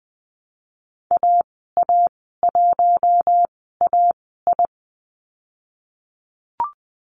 Morse code, heard as a keyed tone, about 20 wpm, spaced as for 14 wpm, 700 Hz. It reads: AA1AI
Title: 042 Lesson 8g - Number 1 - Call Signs 20wpm